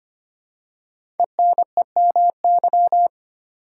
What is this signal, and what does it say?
Morse code, heard as a keyed tone, about 25 wpm, 700 Hz. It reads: ENEMY
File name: 5 Letter Words - First 3 Letters Given - 25wpm